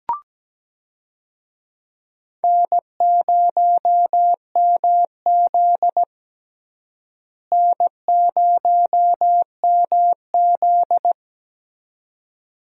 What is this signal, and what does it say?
Morse code, heard as a keyed tone, about 17 wpm, 700 Hz. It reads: N0MZ N0MZ
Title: USA Call Signs - 4 Characters - Repeated 2x - 17wpm